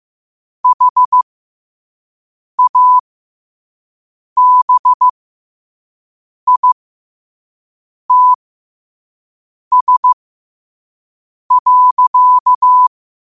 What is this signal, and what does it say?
Morse code, heard as a keyed tone, about 15 wpm, spaced as for 5 wpm, 1000 Hz. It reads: HABITS.